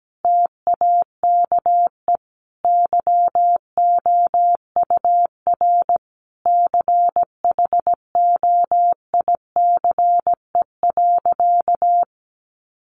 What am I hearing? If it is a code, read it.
Morse code, heard as a keyed tone, about 17 wpm, 700 Hz. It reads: TAKE YOUR CHOICE.
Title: Sentences from Top 900 Words 17wpm